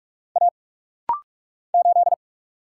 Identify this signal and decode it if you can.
Morse code, heard as a keyed tone, about 46 wpm, 700 Hz. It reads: A 8
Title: Single Letter-Number Alt - Rapid-Fire 45wpm